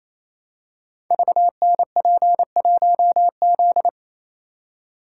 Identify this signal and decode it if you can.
Morse code, heard as a keyed tone, about 27 wpm, 700 Hz. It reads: VNP1Z
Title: Code Groups - 5 Letter:Numbers - 28wpm